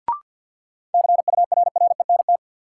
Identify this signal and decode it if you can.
Morse code, heard as a keyed tone, about 51 wpm, 700 Hz. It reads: CURRENT
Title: Sentences from Top 1000 Words - Review Part 4 - New Word Drill - 50wpm